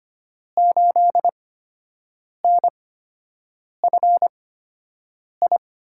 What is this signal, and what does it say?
Morse code, heard as a keyed tone, about 25 wpm, spaced as for 7 wpm, 700 Hz. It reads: 8NFI
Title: USA Call Signs - 5 Characters - 8x Character Spacing - 25wpm